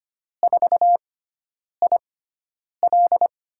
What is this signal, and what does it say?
Morse code, heard as a keyed tone, about 25 wpm, spaced as for 9 wpm, 700 Hz. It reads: 4IL
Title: USA Call Signs - 5 Characters - 6x Character Spacing - 25wpm